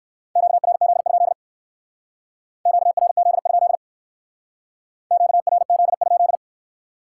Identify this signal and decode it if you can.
Morse code, heard as a keyed tone, about 58 wpm, 700 Hz. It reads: QRZ? QRZ? QRZ?